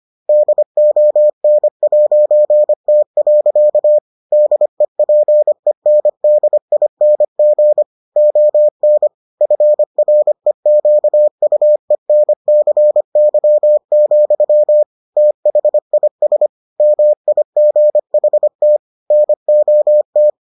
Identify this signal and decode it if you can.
Morse code, heard as a keyed tone, about 25 wpm, 600 Hz. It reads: DON'T. DEPENDING ON FREQUENCY, THIS MIGHT NOT